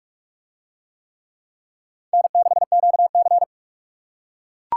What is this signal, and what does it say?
Morse code, heard as a keyed tone, about 44 wpm, 700 Hz. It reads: N6QC